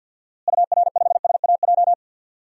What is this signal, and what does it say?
Morse code, heard as a keyed tone, about 52 wpm, 700 Hz. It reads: UR5SAJ
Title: DXCC-100 Call Signs - 6 Characters - Repeated 3x - 50wpm